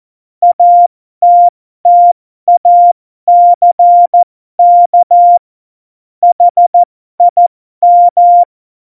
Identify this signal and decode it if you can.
Morse code, heard as a keyed tone, about 14 wpm, spaced as for 12 wpm, 700 Hz. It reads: ATTACK HIM
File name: Morsekurs / Witze